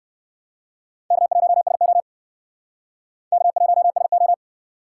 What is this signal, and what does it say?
Morse code, heard as a keyed tone, about 65 wpm, 700 Hz. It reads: K1SO K1SO